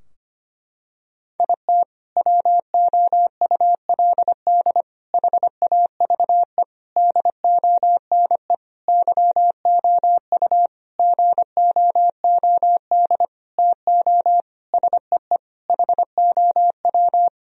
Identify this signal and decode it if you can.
Morse code, heard as a keyed tone, about 25 wpm, 700 Hz. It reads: IT WOULD HAVE DONE YOU GOOD TO SEE HOW
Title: Sentences from Top 1000 Words - Review Part 2 - 25wpm